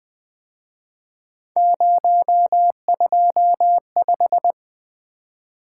Morse code, heard as a keyed tone, about 24 wpm, 700 Hz. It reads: 025
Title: Speed-Racing - Three Letter-Number Combinations - 20wpm